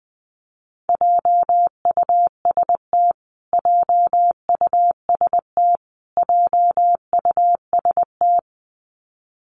Morse code, heard as a keyed tone, about 20 wpm, 700 Hz. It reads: JUST JUST JUST
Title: Top 200 Words Repeated 20wpm